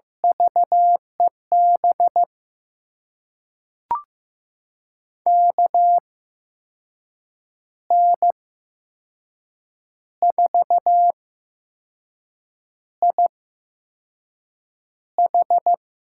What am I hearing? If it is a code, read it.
Morse code, heard as a keyed tone, about 15 wpm, spaced as for 5 wpm, 700 Hz. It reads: VEB KN4IH
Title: USA Call Signs - 6 Characters - 8x Character Spacing - 15wpm